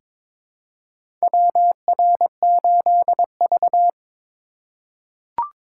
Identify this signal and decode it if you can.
Morse code, heard as a keyed tone, about 22 wpm, 700 Hz. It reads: WR8V
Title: USA Call Signs - 4 Characters - 2x Character Spacing - 22wpm